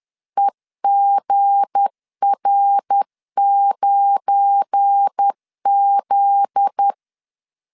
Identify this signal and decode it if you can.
Morse code, heard as a keyed tone, about 10 wpm, 800 Hz. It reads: EGR9Z
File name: LEZIONI / 8 YJ45 / 8 YJ45 10wpm 2549